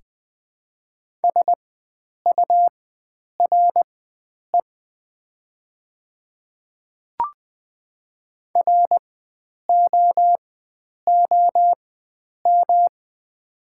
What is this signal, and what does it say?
Morse code, heard as a keyed tone, about 20 wpm, spaced as for 9 wpm, 700 Hz. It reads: SURE ROOM
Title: 4x Character Spacing - Top 300 Words - 20wpm